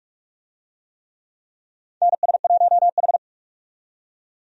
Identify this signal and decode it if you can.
Morse code, heard as a keyed tone, about 46 wpm, 700 Hz. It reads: NS1H
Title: USA Call Signs - 4 Characters - 45wpm